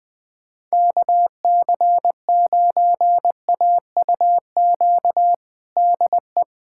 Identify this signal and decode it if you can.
Morse code, heard as a keyed tone, about 20 wpm, 700 Hz. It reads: KC9AUQ DE
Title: USA Callsign DE Callsign - 6 Characters - 20wpm